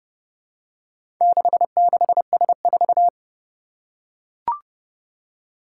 Morse code, heard as a keyed tone, about 30 wpm, 700 Hz. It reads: 66S4
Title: Four Challenging Letter-Number Combinations 30wpm